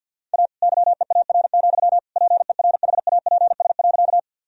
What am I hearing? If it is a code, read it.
Morse code, heard as a keyed tone, about 52 wpm, 700 Hz. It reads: A YEAR, PERHAPS.
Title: Sentences from Top 1000 Words - Review Part 1 - 50wpm